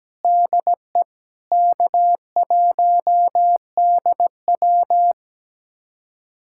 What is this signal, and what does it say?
Morse code, heard as a keyed tone, about 17 wpm, 700 Hz. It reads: DE K1DW
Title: USA Callsign DE Callsign - 4 Characters - 17wpm